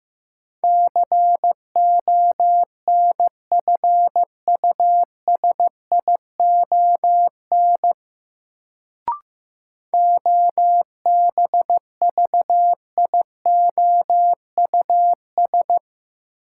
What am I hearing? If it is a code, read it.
Morse code, heard as a keyed tone, about 15 wpm, 700 Hz. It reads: CONFUSION OBVIOUS
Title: Sentences from Top 1600 Words - New Word Drill Alt - 15wpm